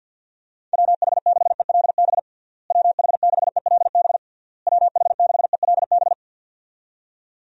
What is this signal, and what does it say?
Morse code, heard as a keyed tone, about 52 wpm, 700 Hz. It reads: WH6ELB WH6ELB WH6ELB
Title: USA Call Signs - 6 Characters - Repeated 3x - 50wpm